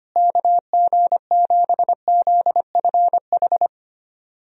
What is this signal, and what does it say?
Morse code, heard as a keyed tone, about 25 wpm, 700 Hz. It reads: KG7ZFH
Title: USA Call Signs - 6 Characters - Repeated 2x - 25wpm